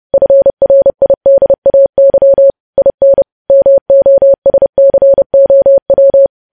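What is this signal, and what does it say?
Morse code, heard as a keyed tone, about 30 wpm, 550 Hz. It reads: FRIDAY IN MOSCOW
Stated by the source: News headlines sent in Morse code at 30.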